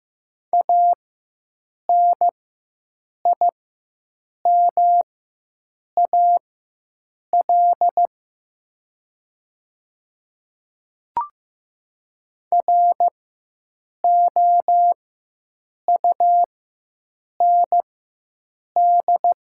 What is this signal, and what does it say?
Morse code, heard as a keyed tone, about 15 wpm, spaced as for 7 wpm, 700 Hz. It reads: ANIMAL ROUND
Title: 4x Character Spacing - Top 200 Words - 15wpm